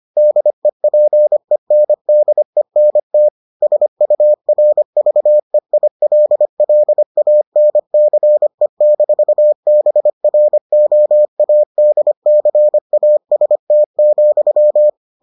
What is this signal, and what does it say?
Morse code, heard as a keyed tone, about 25 wpm, 600 Hz. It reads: DEPENDENT SURVEILLANCE-BROADCAST,